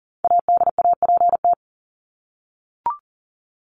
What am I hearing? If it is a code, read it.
Morse code, heard as a keyed tone, about 40 wpm, 700 Hz. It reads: ADAPT
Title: Top 1501-1750 Words 40wpm